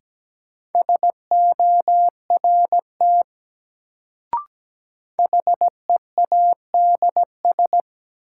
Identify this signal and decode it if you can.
Morse code, heard as a keyed tone, about 17 wpm, 700 Hz. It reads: SORT HEADS
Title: Sentences from Top 1000 Words - Review Part 3 - New Word Drill - 17wpm